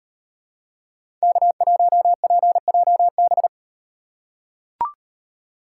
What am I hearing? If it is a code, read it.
Morse code, heard as a keyed tone, about 37 wpm, 700 Hz. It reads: K1PJB